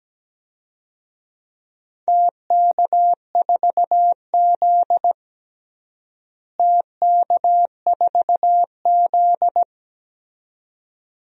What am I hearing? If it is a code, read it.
Morse code, heard as a keyed tone, about 17 wpm, 700 Hz. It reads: TK4Z TK4Z
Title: DXCC-100 Call Signs - 4 Characters - Repeated 2x - 17wpm